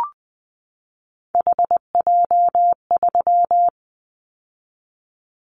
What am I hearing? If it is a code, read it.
Morse code, heard as a keyed tone, about 20 wpm, 700 Hz. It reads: HJ3
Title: Callsign Prefixes for Top 100 DXCC - 20wpm